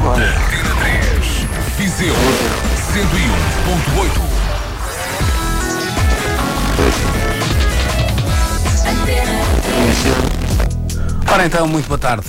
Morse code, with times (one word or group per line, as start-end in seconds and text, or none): none